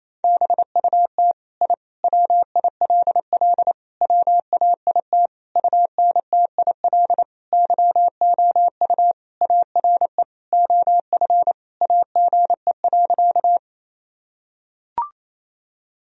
Morse code, heard as a keyed tone, about 28 wpm, 700 Hz. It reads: BUT I WILL WAIT UNTIL YOU ARE OF AGE.
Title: Sentences from Top 1000 Words - Review Part 3 - 28wpm